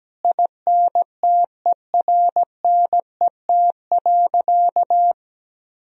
Morse code, heard as a keyed tone, about 17 wpm, 700 Hz. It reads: INTERNET.
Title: Sentences from Top 1000 Words - Review Part 4 - 17wpm